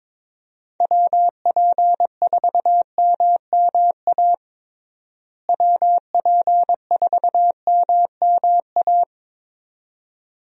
Morse code, heard as a keyed tone, about 22 wpm, 700 Hz. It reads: WP4MMA WP4MMA
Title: USA Call Signs - 6 Characters - Repeated 2x - 22wpm